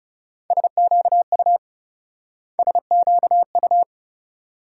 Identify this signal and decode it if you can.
Morse code, heard as a keyed tone, about 38 wpm, 700 Hz. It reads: SQU SQU